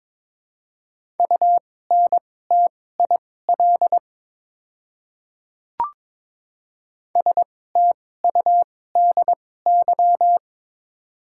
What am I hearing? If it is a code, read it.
Morse code, heard as a keyed tone, about 22 wpm, spaced as for 16 wpm, 700 Hz. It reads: UNTIL STUDY
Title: 2x Character Spacing - Top 500 Words - 22wpm